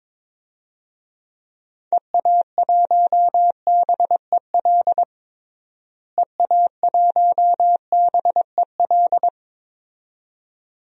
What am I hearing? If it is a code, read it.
Morse code, heard as a keyed tone, about 22 wpm, 700 Hz. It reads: EA1BEL EA1BEL
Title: DXCC-100 Call Signs - 6 Characters - Repeated 2x - 22wpm